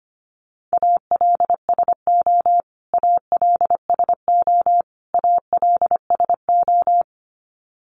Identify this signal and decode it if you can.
Morse code, heard as a keyed tone, about 25 wpm, 700 Hz. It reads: ALSO ALSO ALSO
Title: Top 300 Words Repeated 25wpm